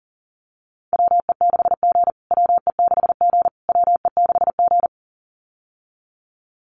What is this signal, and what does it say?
Morse code, heard as a keyed tone, about 40 wpm, 700 Hz. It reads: WE6G WE6G WE6G
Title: USA Call Signs Repeated 3x 40wpm